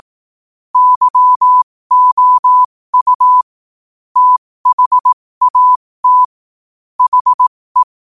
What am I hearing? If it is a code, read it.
Morse code, heard as a keyed tone, about 18 wpm, spaced as for 15 wpm, 1000 Hz. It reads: YOU THAT HE
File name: qotdfiles / 15wpm